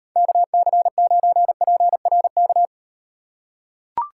Morse code, heard as a keyed tone, about 38 wpm, 700 Hz. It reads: KC9PRK